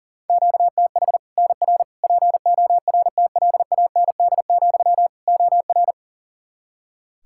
Morse code, heard as a keyed tone, about 37 wpm, 700 Hz. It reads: QTH NR PORTLAND, OR
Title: Practice QSO - Exchange 1 - Beginning - 40wpm